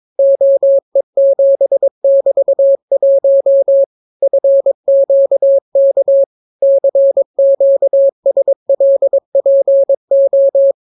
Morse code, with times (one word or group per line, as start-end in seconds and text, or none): 0.2-3.8 OE7<BT>1
4.2-6.2 FQK
6.6-10.7 CQSLPO